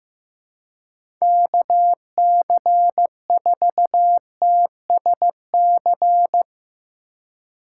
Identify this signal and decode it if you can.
Morse code, heard as a keyed tone, about 15 wpm, 700 Hz. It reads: KC4TSC
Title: USA Call Signs - 6 Characters - 2x Character Spacing - 15wpm